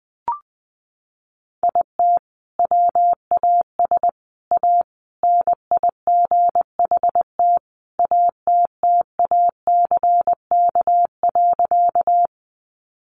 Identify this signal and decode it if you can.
Morse code, heard as a keyed tone, about 20 wpm, 700 Hz. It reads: IT WAS A NIGHT ATTACK.
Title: Sentences from Top 1500 Words 20wpm